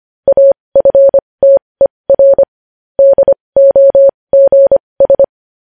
Morse code, heard as a keyed tone, about 25 wpm, spaced as for 20 wpm, 550 Hz. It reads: AFTER DOGS
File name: podcast / News headlines-1x-25x20